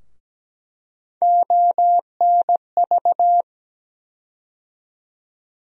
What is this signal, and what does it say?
Morse code, heard as a keyed tone, about 17 wpm, 700 Hz. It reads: ONV
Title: USA Call Sign Postfixes - 3 Letters 17wpm